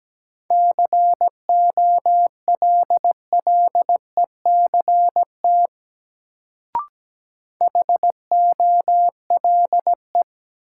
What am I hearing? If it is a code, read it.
Morse code, heard as a keyed tone, about 17 wpm, 700 Hz. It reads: COLLECT HOLE
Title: Sentences from Top 1600 Words - New Word Drill Alt - 17wpm